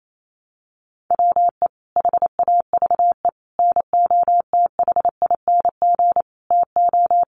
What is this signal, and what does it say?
Morse code, heard as a keyed tone, about 28 wpm, 700 Hz. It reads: WE HAVE NOTHING TO
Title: Sentences from Top 1100 Words 28wpm